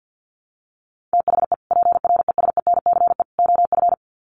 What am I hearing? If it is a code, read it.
Morse code, heard as a keyed tone, about 50 wpm, 700 Hz. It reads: THE PRESENCE OF
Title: Sets of 3 Words 50wpm